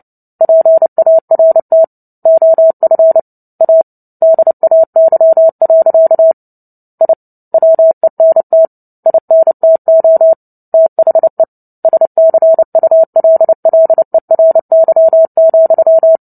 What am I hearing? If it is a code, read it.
Morse code, heard as a keyed tone, about 30 wpm, 650 Hz. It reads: PART OF A DAY. I WENT INTO THE SCULLERY,
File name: Book Two / 03-Chapter 3 - 29 WPM